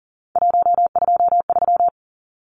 Morse code, heard as a keyed tone, about 40 wpm, 700 Hz. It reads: 123
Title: Numbers - 3-Digit 40wpm